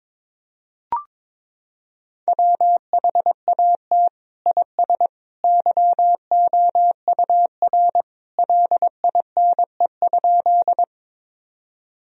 Morse code, heard as a keyed tone, about 22 wpm, 700 Hz. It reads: WHAT IS YOUR LINE?
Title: Sentences from Top 1000 Words - Review Part 1 - 22wpm